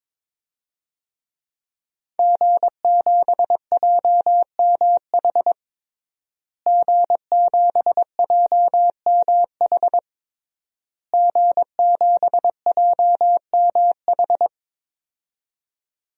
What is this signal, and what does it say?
Morse code, heard as a keyed tone, about 22 wpm, 700 Hz. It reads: G7JMH G7JMH G7JMH
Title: DXCC-100 Call Signs - 5 Characters - Repeated 3x - 22wpm